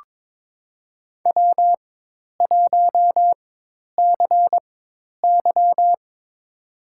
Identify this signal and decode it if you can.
Morse code, heard as a keyed tone, about 22 wpm, spaced as for 10 wpm, 700 Hz. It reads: W1CY
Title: USA Call Signs - 4 Characters - 4x Character Spacing - 22wpm